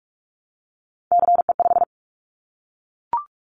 Morse code, heard as a keyed tone, about 46 wpm, 700 Hz. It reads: CE5
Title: Callsign Prefixes for Top 100 DXCC - 45wpm